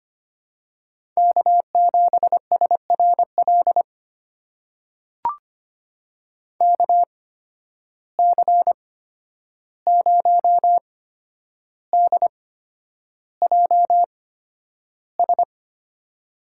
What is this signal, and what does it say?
Morse code, heard as a keyed tone, about 25 wpm, spaced as for 10 wpm, 700 Hz. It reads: K7SRL KC0DJS
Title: USA Call Signs - 8x Character Spacing - 25wpm